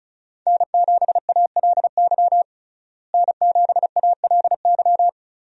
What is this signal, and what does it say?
Morse code, heard as a keyed tone, about 34 wpm, 700 Hz. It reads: N7ALY N7ALY
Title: USA Call Signs - 5 Characters - Repeated 2x - 35wpm